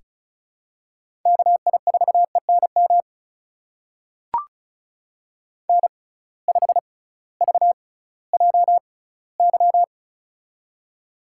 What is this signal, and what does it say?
Morse code, heard as a keyed tone, about 36 wpm, spaced as for 19 wpm, 700 Hz. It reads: KI4ENM N5VJY
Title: USA Call Signs - 6x Character Spacing - 35wpm